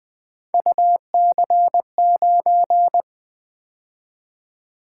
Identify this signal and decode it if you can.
Morse code, heard as a keyed tone, about 24 wpm, 700 Hz. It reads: UC9
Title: Speed-Racing - Three Letter-Number Combinations - 20wpm